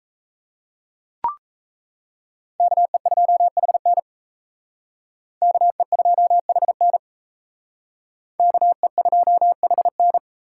Morse code, heard as a keyed tone, about 35 wpm, 700 Hz. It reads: KE2HN KE2HN KE2HN